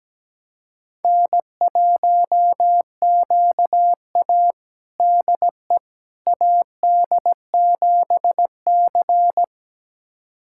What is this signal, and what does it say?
Morse code, heard as a keyed tone, about 17 wpm, 700 Hz. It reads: N1QA DE AD7C
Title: USA Callsign DE Callsign - 4 Characters - 17wpm